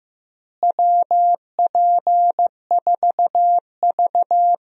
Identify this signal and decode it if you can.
Morse code, heard as a keyed tone, about 15 wpm, 700 Hz. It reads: WP4V
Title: USA Call Signs - 4 Characters - Repeated 3x - 15wpm